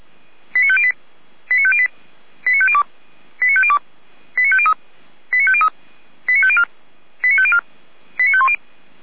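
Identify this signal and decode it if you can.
Five-tone selective call (ZVEI), heard as quick runs of five tones, tones of about 70 ms. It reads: 89508 89508 89502 89502 89503 89503 89504 89504 89510
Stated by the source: ZVEI five-tone call 89167 received on SDR.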